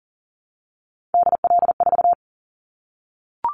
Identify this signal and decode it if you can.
Morse code, heard as a keyed tone, about 40 wpm, 700 Hz. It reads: DL4
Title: Callsign Prefixes for Top 50 DXCC - 40wpm